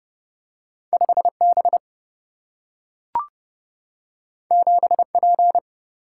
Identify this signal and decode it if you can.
Morse code, heard as a keyed tone, about 30 wpm, 700 Hz. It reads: 5B 7P
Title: Single-Letter Single-Number 30wpm